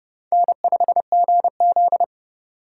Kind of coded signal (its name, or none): Morse code